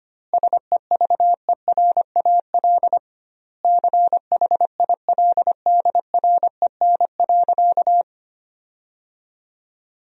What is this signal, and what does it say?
Morse code, heard as a keyed tone, about 25 wpm, 700 Hz. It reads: SEVERAL CHILDREN.
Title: 2x Word-Spacing - Sentences from Top 400 Words - 25wpm